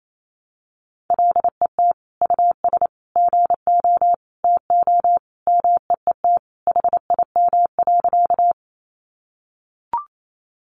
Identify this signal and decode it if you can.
Morse code, heard as a keyed tone, about 28 wpm, 700 Hz. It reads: LET US GO TO MEET HIM.
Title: Sentences from Top 1100 Words 28wpm